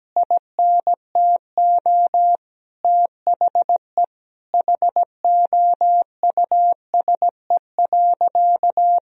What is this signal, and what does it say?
Morse code, heard as a keyed tone, about 17 wpm, 700 Hz. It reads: INTO THE HOUSE.
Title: Sentences from Top 1000 Words - Review Part 4 - 17wpm